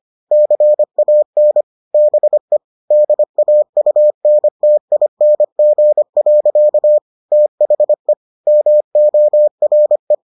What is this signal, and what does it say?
Morse code, heard as a keyed tone, about 25 wpm, 600 Hz. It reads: CAN BE DAUNTING. THE MORE